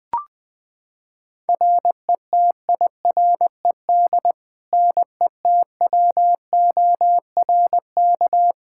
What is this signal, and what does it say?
Morse code, heard as a keyed tone, about 21 wpm, 700 Hz. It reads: RETIRED NETWORK
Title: K1USN SST QSO Practice 20wpm